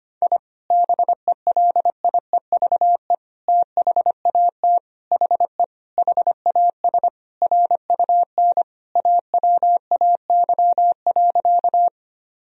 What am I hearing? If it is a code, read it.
Morse code, heard as a keyed tone, about 25 wpm, 700 Hz. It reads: I BELIEVE THAT HE HAS RUN AWAY.
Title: Sentences from Top 1000 Words - Review Part 1 - 25wpm